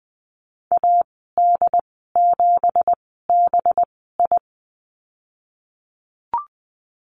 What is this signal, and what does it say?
Morse code, heard as a keyed tone, about 19 wpm, spaced as for 14 wpm, 700 Hz. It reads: AD7BI
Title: 186 Lesson 30g - Letter b - Call Signs 20wpm